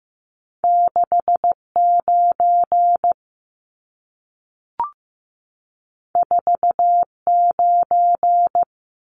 Morse code, heard as a keyed tone, about 15 wpm, 700 Hz. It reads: 69 49
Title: Numbers - 2-Digit 15wpm